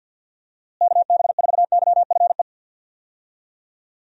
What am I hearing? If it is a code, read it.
Morse code, heard as a keyed tone, about 47 wpm, 700 Hz. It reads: KB4YFE
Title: USA Call Signs - 6 Characters - Repeated 2x - 50wpm